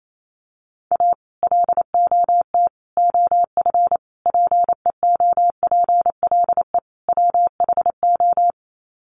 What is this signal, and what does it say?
Morse code, heard as a keyed tone, about 28 wpm, 700 Hz. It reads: A LOT OF PEOPLE WHO
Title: Sets of 5 Words 28wpm